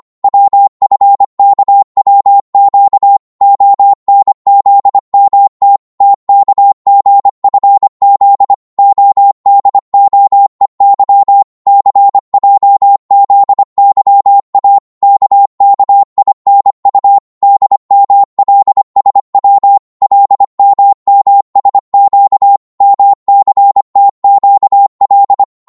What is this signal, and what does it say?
Morse code, heard as a keyed tone, about 25 wpm, 800 Hz. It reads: WFKWQ ONZMT TKGFZ ODOEY CJZYA KKINU DMLSW LMMSQ MCTQL